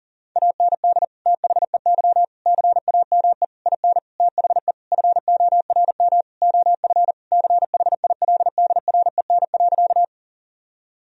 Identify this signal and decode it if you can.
Morse code, heard as a keyed tone, about 40 wpm, 700 Hz. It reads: AND THEY CAME IN THE FORM OF CHILDREN.
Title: Sentences from Top 1000 Words - Review Part 1 - 40wpm